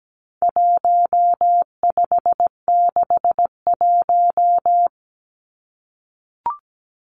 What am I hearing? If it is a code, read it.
Morse code, heard as a keyed tone, about 17 wpm, 700 Hz. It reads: 1561
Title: Numbers - 4-Digit 17wpm